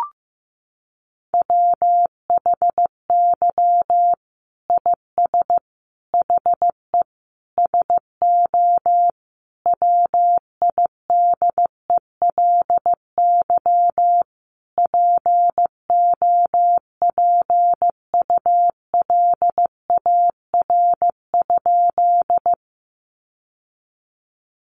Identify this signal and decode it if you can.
Morse code, heard as a keyed tone, about 15 wpm, 700 Hz. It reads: WHY IS HE SO WIDELY POPULAR?
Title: Sentences from Top 1200 Words 15wpm